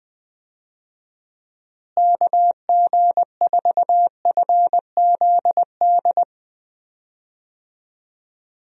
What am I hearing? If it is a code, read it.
Morse code, heard as a keyed tone, about 20 wpm, 700 Hz. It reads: KG4FZD
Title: USA Call Signs - 6 Characters - Repeated 3x - 20wpm